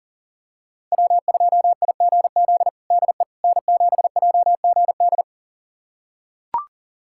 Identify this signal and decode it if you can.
Morse code, heard as a keyed tone, about 40 wpm, 700 Hz. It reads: W2IGZ DE N7JGD
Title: USA Callsign DE Callsign - 5 Characters - 40wpm